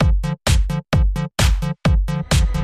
There is no code.